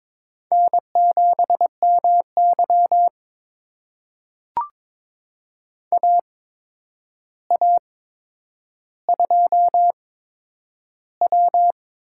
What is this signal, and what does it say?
Morse code, heard as a keyed tone, about 22 wpm, spaced as for 9 wpm, 700 Hz. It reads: N7MY AA2W
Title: USA Call Signs - 4 Characters - 8x Character Spacing - 22wpm